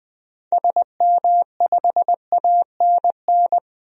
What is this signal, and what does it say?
Morse code, heard as a keyed tone, about 20 wpm, 700 Hz. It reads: SM5ANN
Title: DXCC-100 Call Signs - 6 Characters - 20wpm